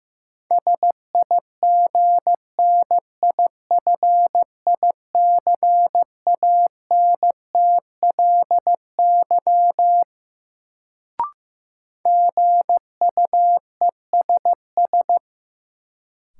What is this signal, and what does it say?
Morse code, heard as a keyed tone, about 15 wpm, 700 Hz. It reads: SIGNIFICANTLY GUESS
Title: Sentences from Top 1500 Words - New Word Drill Alt - 15wpm